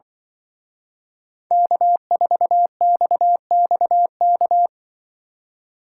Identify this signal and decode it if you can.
Morse code, heard as a keyed tone, about 23 wpm, 700 Hz. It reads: K4XXK
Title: Speed Racing - USA 1x3 Call Signs 20wpm